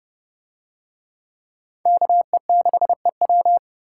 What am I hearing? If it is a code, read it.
Morse code, heard as a keyed tone, about 29 wpm, 700 Hz. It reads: KE6EW